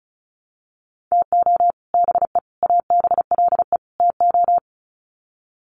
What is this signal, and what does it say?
Morse code, heard as a keyed tone, about 34 wpm, 700 Hz. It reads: TO BE ABLE TO
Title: Sets of 4 Words 35wpm